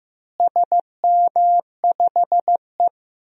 Morse code, heard as a keyed tone, about 15 wpm, 700 Hz. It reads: SM5E
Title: DXCC-100 Call Signs - 4 Characters - Repeated 2x - 15wpm